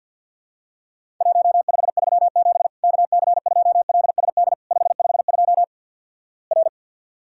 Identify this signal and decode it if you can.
Morse code, heard as a keyed tone, about 50 wpm, 700 Hz. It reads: 1537 KC2LSD 552 R
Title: Worked All Europe DX Contest - QTC Exchange - 50wpm